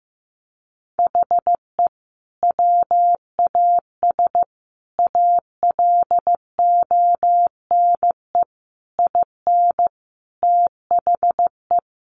Morse code, heard as a keyed tone, about 15 wpm, 700 Hz. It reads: HE WAS ALONE IN THE
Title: Sentences from Top 1000 Words 15wpm